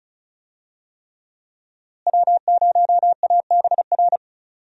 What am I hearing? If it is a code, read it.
Morse code, heard as a keyed tone, about 36 wpm, 700 Hz. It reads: W0ABR
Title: USA Call Signs - 5 Characters - Repeated 3x - 35wpm